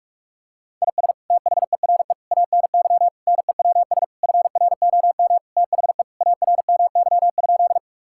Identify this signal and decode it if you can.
Morse code, heard as a keyed tone, about 46 wpm, 700 Hz. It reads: IS THERE ANY NEWS FROM THE ARMY?
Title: Sentences from Top 1000 Words - Review Part 5 - 45wpm